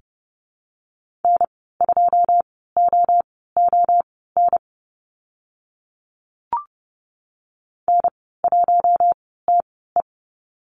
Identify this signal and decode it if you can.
Morse code, heard as a keyed tone, about 30 wpm, spaced as for 17 wpm, 700 Hz. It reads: N2OON N1TE
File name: Lesson 14g - Number 2 - Call Signs 30wpm